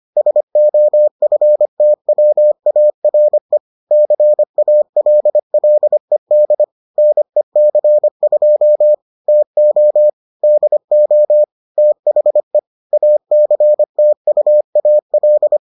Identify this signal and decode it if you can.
Morse code, heard as a keyed tone, about 25 wpm, 600 Hz. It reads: SOFTWARE CALLED NEC2 TO DO THE ACTUAL